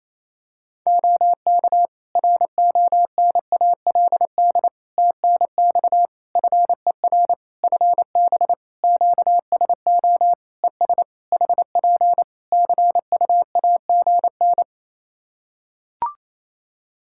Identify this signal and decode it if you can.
Morse code, heard as a keyed tone, about 28 wpm, 700 Hz. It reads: OK RONALD TNX FER FB QSO ES HP CUAGN
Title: Practice QSO - Exchange 4 - Ending - 28wpm